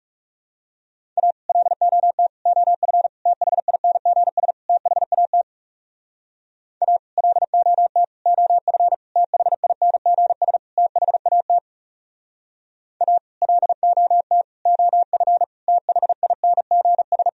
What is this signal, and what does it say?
Morse code, heard as a keyed tone, about 35 wpm, 700 Hz. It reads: A LOT OF THINGS THAT A LOT OF THINGS THAT A LOT OF THINGS
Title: Speed-Racing - Sets of 5 Words 30wpm